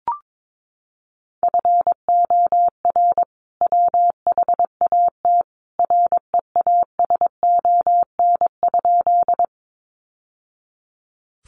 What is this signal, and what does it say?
Morse code, heard as a keyed tone, about 22 wpm, 700 Hz. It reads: FOR WHAT REASON?
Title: Sentences from Top 700 Words 22wpm